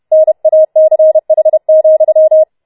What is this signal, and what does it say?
Morse code, heard as a keyed tone, about 31 wpm, 630 Hz. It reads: NACH,